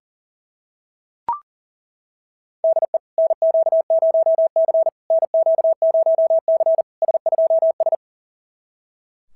Morse code, heard as a keyed tone, about 40 wpm, 690 Hz. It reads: DE NQ0C NQ0C S2S
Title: SOTA QSO Practice 40wpm